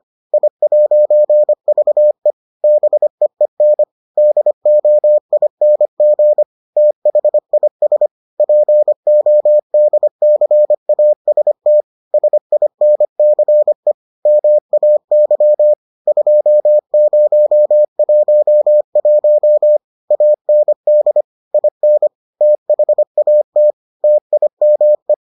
Morse code, heard as a keyed tone, about 25 wpm, 600 Hz. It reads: I'VE BEEN DOING THIS PODCAST SINCE MAY 2011 AND IN THAT TIME